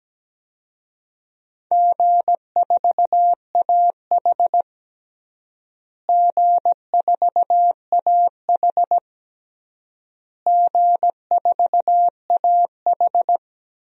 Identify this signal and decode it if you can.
Morse code, heard as a keyed tone, about 17 wpm, 700 Hz. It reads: G4AH G4AH G4AH
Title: DXCC-100 Call Signs - 4 Characters - Repeated 3x - 17wpm